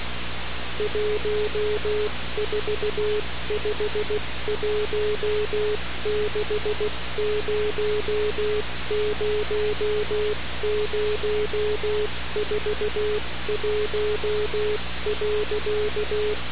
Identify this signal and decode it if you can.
Morse code, heard as a keyed tone, about 16 wpm, 420 Hz, with noise. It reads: 1451600041.
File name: radiocap / REC z